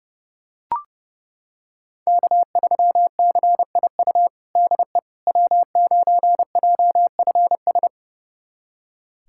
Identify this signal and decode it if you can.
Morse code, heard as a keyed tone, about 30 wpm, 700 Hz. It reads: K3CIU DE W9JFS
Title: USA Callsign DE Callsign - 5 Characters - 30wpm